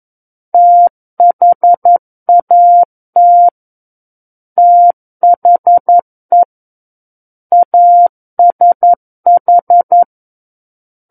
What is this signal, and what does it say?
Morse code, heard as a keyed tone, about 11 wpm, 700 Hz. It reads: THAT THE ASH
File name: Book One / 02-Chapter 2 - 11 WPM